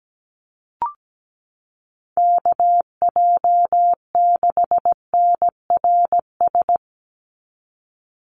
Morse code, heard as a keyed tone, about 17 wpm, 700 Hz. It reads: KJ6NRS
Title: USA Call Signs 17wpm